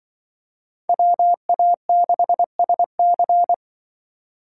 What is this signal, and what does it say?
Morse code, heard as a keyed tone, about 24 wpm, 700 Hz. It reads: WA6SC
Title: Speed Racing - USA 2x2 Call Signs 20wpm